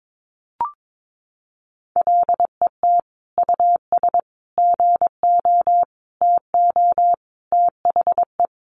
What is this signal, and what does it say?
Morse code, heard as a keyed tone, about 22 wpm, 700 Hz. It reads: LET US GO TO THE
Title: Sentences from Top 1400 Words 22wpm